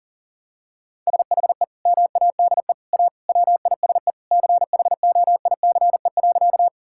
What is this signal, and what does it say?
Morse code, heard as a keyed tone, about 40 wpm, 700 Hz. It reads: SHE MADE A WISE CHOICE.
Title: Sentences from Top 1600 Words 40wpm